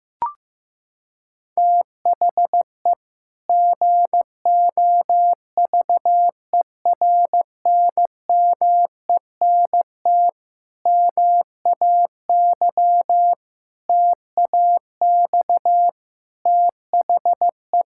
Morse code, heard as a keyed tone, about 15 wpm, 700 Hz. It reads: THE GOVERNMENT MAY TAX THE
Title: Sentences from Top 1000 Words - Review Part 5 - 15wpm